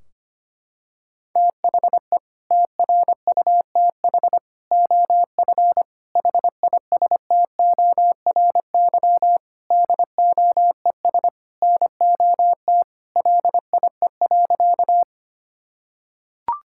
Morse code, heard as a keyed tone, about 25 wpm, 700 Hz. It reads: THE TRUTH OF HISTORY DOES NOT LIE.
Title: Sentences from Top 1600 Words 25wpm